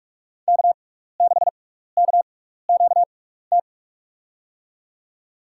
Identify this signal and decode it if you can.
Morse code, heard as a keyed tone, about 45 wpm, spaced as for 15 wpm, 690 Hz. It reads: K6KQT
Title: USA Call Signs - 5 Characters - 6x Character Spacing - 45wpm